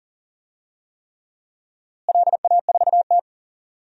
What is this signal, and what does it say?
Morse code, heard as a keyed tone, about 40 wpm, 700 Hz. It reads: LA4T
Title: DXCC-100 Call Signs - 4 Characters - Repeated 3x - 40wpm